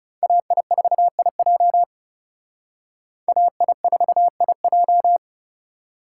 Morse code, heard as a keyed tone, about 37 wpm, 700 Hz. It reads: AI4IJ AI4IJ
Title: Speed Racing - USA 2x2 Call Signs 30wpm